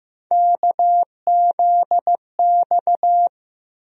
Morse code, heard as a keyed tone, about 15 wpm, 700 Hz. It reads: KZX
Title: USA Call Sign Postfixes - 3 Letters 15wpm